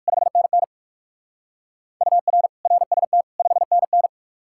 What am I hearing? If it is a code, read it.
Morse code, heard as a keyed tone, about 56 wpm, 700 Hz. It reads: 5NN UR RST 5NN
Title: Speed-Racing Common QSO Phrases - 45wpm